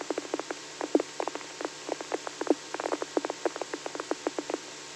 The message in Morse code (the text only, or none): none